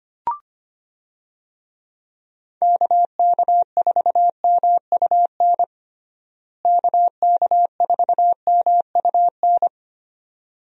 Morse code, heard as a keyed tone, about 25 wpm, 700 Hz. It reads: KK4MUN KK4MUN
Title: USA Call Signs - 6 Characters - Repeated 2x - 25wpm